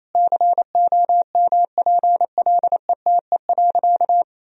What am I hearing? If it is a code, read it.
Morse code, heard as a keyed tone, about 27 wpm, 700 Hz. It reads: COMPLETE.
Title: Sentences from Top 400 Words and Abbreviations 28wpm